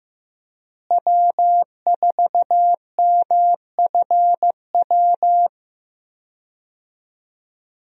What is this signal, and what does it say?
Morse code, heard as a keyed tone, about 15 wpm, 700 Hz. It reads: W4MFW
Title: USA Call Signs - 5 Characters - Repeated 2x - 15wpm